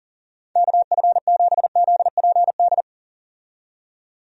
Morse code, heard as a keyed tone, about 40 wpm, 700 Hz. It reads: KF7ZPD
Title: USA Call Signs - 6 Characters - 40wpm